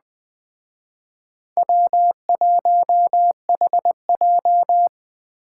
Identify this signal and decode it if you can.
Morse code, heard as a keyed tone, about 20 wpm, 700 Hz. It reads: W1HJ
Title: Speed Racing - USA 1x2 Call Signs 17wpm